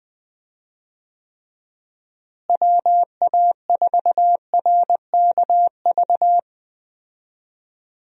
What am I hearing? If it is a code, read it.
Morse code, heard as a keyed tone, about 20 wpm, 700 Hz. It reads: WA4RKV
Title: USA Call Signs - 6 Characters - Repeated 3x - 20wpm